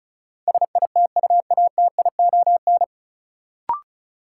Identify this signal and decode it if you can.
Morse code, heard as a keyed tone, about 36 wpm, 700 Hz. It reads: SITUATION
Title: Sentences from Top 1200 Words - New Word Drill - 35wpm